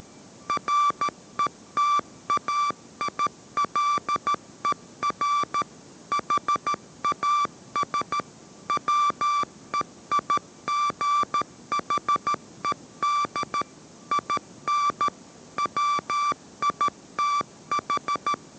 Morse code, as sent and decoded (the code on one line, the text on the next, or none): .-. - .- .. .-.. .-. / .... .- ... / .-- . .. --. .... -.. / .. -. / .-- .. - ....
RETAILER HAS WEIGHED IN WITH